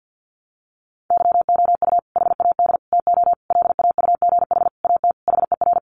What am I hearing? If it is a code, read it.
Morse code, heard as a keyed tone, about 50 wpm, 700 Hz. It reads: YOU HAD TO LAUGH AT HER
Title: Sentences from Top 600 Words 50wpm